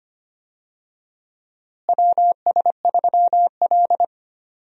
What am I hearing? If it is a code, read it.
Morse code, heard as a keyed tone, about 25 wpm, 700 Hz. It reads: WS3L
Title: USA Call Signs - 4 Characters - Repeated 2x - 25wpm